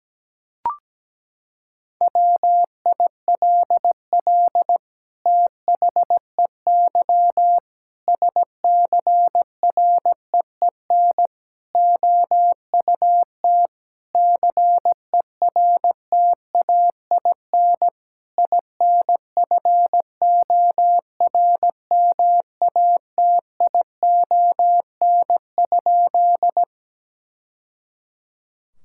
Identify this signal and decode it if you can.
Morse code, heard as a keyed tone, about 17 wpm, 700 Hz. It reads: WILL THEY SCREEN OUT CERTAIN INFORMATION?
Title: Sentences from Top 1000 Words - Review Part 6 - 17wpm